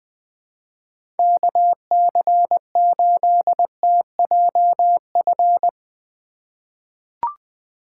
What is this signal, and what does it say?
Morse code, heard as a keyed tone, about 20 wpm, 700 Hz. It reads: KC8TJF
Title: USA Call Signs - 6 Characters - 4x Character Spacing - 20wpm